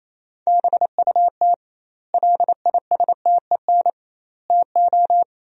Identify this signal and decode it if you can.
Morse code, heard as a keyed tone, about 27 wpm, 700 Hz. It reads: BUT LISTEN TO
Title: 2x Word-Spacing - Sentences from Top 400 Words - 28wpm